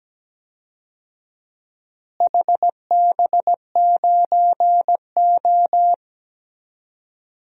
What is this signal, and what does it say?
Morse code, heard as a keyed tone, about 17 wpm, 700 Hz. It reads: HB9O
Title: DXCC-100 Call Signs - 4 Characters - Repeated 3x - 17wpm